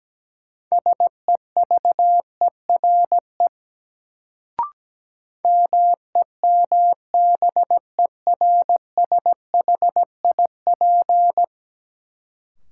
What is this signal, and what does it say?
Morse code, heard as a keyed tone, about 17 wpm, 700 Hz. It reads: SEVERE MEMBERSHIP
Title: Sentences from Top 1400 Words - New Word Drill Alt - 17wpm